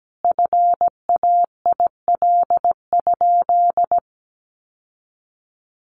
Morse code, heard as a keyed tone, about 17 wpm, 700 Hz. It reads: FAIL?
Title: Sentences from Top 1100 Words 17wpm